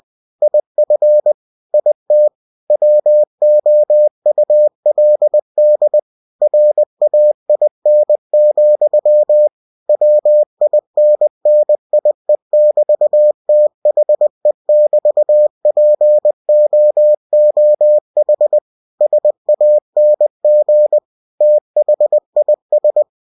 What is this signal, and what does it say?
Morse code, heard as a keyed tone, about 20 wpm, 600 Hz. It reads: IF IT WOULD RAIN, WINNIE<BT>THE<BT>POOH SANG THIS